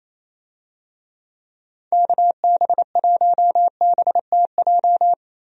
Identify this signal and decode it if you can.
Morse code, heard as a keyed tone, about 27 wpm, 700 Hz. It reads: KB1BTJ
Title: USA Call Signs - 6 Characters - 28wpm